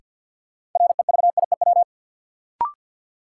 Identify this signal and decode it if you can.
Morse code, heard as a keyed tone, about 52 wpm, 700 Hz. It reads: REVIEW
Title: Sentences from Top 900 Words - New Word Drill - 50wpm